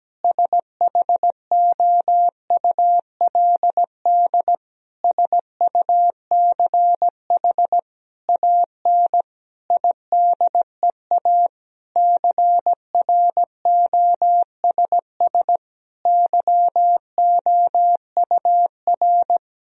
Morse code, heard as a keyed tone, about 17 wpm, 700 Hz. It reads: SHOULD SUCH AN IDEA CROSS YOUR
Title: Sentences from Top 1000 Words - Review Part 1 - 17wpm